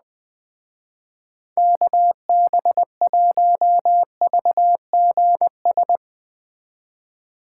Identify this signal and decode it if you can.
Morse code, heard as a keyed tone, about 20 wpm, 700 Hz. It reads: KB1VGS